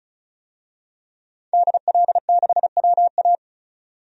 Morse code, heard as a keyed tone, about 36 wpm, 700 Hz. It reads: DL6WA
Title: DXCC-100 Call Signs - 5 Characters - 35wpm